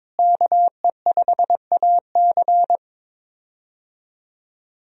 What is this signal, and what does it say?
Morse code, heard as a keyed tone, about 22 wpm, 700 Hz. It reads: KE5AC
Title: USA Call Signs - 5 Characters - Repeated 2x - 22wpm